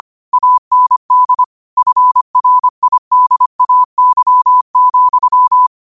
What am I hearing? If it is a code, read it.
Morse code, heard as a keyed tone, about 25 wpm, 1000 Hz. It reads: AND FRIDAY,
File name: qotdfiles / 25wpm